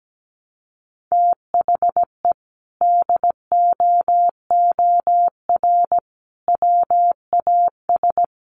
Morse code, heard as a keyed tone, about 17 wpm, 700 Hz. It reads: THE DOOR WAS
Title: Sentences from Top 1300 Words 17wpm